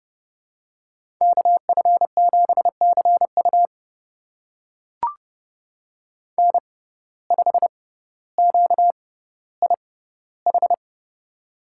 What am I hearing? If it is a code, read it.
Morse code, heard as a keyed tone, about 30 wpm, spaced as for 15 wpm, 700 Hz. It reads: KF7CU N5QIH
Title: USA Call Signs - 5 Characters - 6x Character Spacing - 30wpm